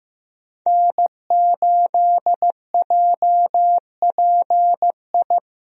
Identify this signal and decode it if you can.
Morse code, heard as a keyed tone, about 15 wpm, 700 Hz. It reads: N8JPI
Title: USA Call Signs - 5 Characters - Repeated 2x - 15wpm